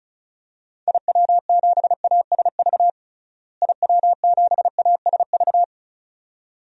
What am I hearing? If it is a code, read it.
Morse code, heard as a keyed tone, about 36 wpm, 700 Hz. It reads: IW7ASV IW7ASV
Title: DXCC-100 Call Signs - 6 Characters - Repeated 2x - 35wpm